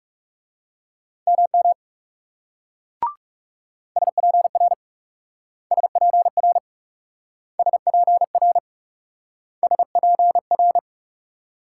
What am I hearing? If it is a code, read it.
Morse code, heard as a keyed tone, about 42 wpm, 700 Hz. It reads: MM SPR SPR SPR SPR